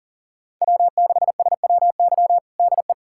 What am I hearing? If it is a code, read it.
Morse code, heard as a keyed tone, about 40 wpm, 700 Hz. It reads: W6SWY DE
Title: USA Callsign DE Callsign 40wpm